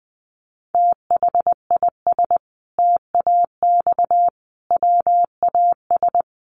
Morse code, heard as a keyed tone, about 20 wpm, 700 Hz. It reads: THIS TAX WAS
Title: Sentences from Top 1500 Words 20wpm